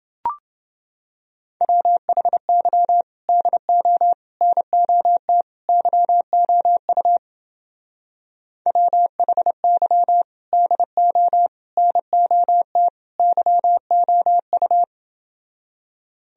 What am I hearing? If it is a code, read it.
Morse code, heard as a keyed tone, about 28 wpm, 700 Hz. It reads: WHY DO NOT YOU WHY DO NOT YOU